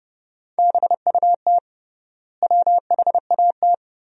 Morse code, heard as a keyed tone, about 30 wpm, 700 Hz. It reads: BUT WHAT
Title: 3x Word-Spacing - Sentences from Top 100 Words - 30wpm